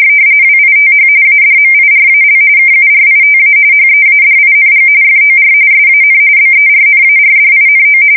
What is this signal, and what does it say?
RTTY, about 45.45 baud. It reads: WITH ADMIRAL KURITA HEADING FOR SAN BERNARDINO
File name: fsk170 / battleshipnj